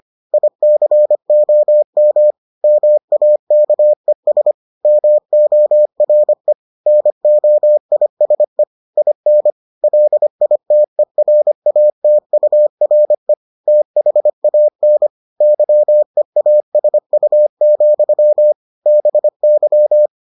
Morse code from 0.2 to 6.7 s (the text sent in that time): ICOM MAKES MORE